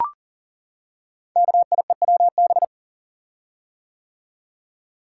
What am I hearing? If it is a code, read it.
Morse code, heard as a keyed tone, about 42 wpm, 690 Hz. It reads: KIEWB
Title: Code Groups - 5 Letters - 40wpm